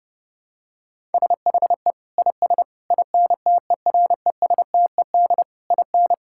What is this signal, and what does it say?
Morse code, heard as a keyed tone, about 30 wpm, 700 Hz. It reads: SHE IS INTERESTED IN
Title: Sentences from Top 1000 Words - Review Part 6 - 30wpm